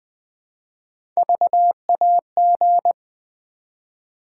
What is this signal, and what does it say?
Morse code, heard as a keyed tone, about 20 wpm, 700 Hz. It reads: VAG
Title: Build A Word - 1 Letter At A Time - 5-Letter Words - 20wpm